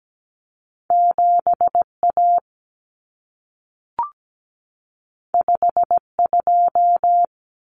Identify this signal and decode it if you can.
Morse code, heard as a keyed tone, about 17 wpm, 700 Hz. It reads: 7A 52
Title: Common Cut Numbers - 2-Digit 17wpm